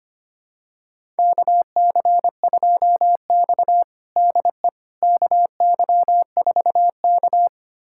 Morse code, heard as a keyed tone, about 25 wpm, 700 Hz. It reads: KC2X DE KY4K
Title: USA Callsign DE Callsign - 4 Characters - 25wpm